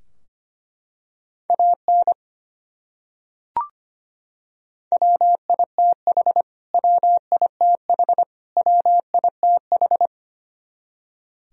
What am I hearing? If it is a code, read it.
Morse code, heard as a keyed tone, about 25 wpm, 700 Hz. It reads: AN WITH WITH WITH
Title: Top 100 Words Repeated 25wpm